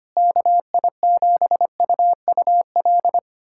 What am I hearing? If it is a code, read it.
Morse code, heard as a keyed tone, about 25 wpm, 700 Hz. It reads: KI7UUL
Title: USA Callsign DE Callsign 25wpm